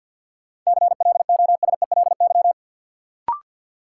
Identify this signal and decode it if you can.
Morse code, heard as a keyed tone, about 52 wpm, 700 Hz. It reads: CLOSELY
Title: Sentences from Top 1400 Words - New Word Drill - 50wpm